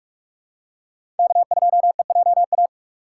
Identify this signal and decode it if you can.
Morse code, heard as a keyed tone, about 41 wpm, 700 Hz. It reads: K2EJA
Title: USA Call Signs - 5 Characters - 2x Character Spacing - 45wpm